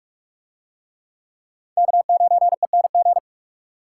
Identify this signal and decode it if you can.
Morse code, heard as a keyed tone, about 44 wpm, 700 Hz. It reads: K9ENG